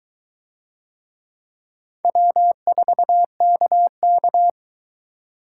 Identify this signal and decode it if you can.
Morse code, heard as a keyed tone, about 22 wpm, 700 Hz. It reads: W4KK